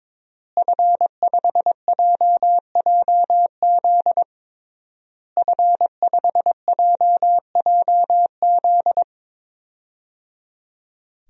Morse code, heard as a keyed tone, about 22 wpm, 700 Hz. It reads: F5JJZ F5JJZ
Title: DXCC-100 Call Signs - 5 Characters - Repeated 3x - 22wpm